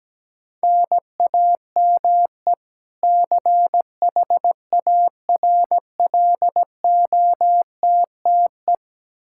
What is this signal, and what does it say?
Morse code, heard as a keyed tone, about 17 wpm, 700 Hz. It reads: NAME CHARLOTTE
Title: Practice QSO - Exchange 1 - Beginning - 17wpm